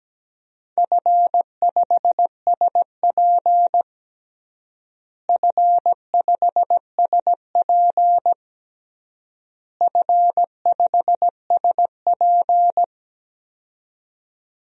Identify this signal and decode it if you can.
Morse code, heard as a keyed tone, about 17 wpm, 700 Hz. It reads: F5SP F5SP F5SP
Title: DXCC-100 Call Signs - 4 Characters - Repeated 3x - 17wpm